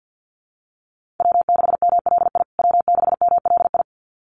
Morse code, heard as a keyed tone, about 50 wpm, 700 Hz. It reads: W6MLI W6MLI
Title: USA Call Signs Repeated 2x 50wpm